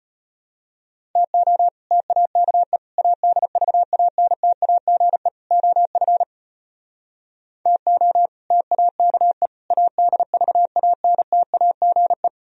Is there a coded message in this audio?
Yes: Morse code from 1.2 to 12.3 s, heard as a keyed tone, about 33 wpm, 700 Hz. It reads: TO TAKE ADVANTAGE OF TO TAKE ADVANTAGE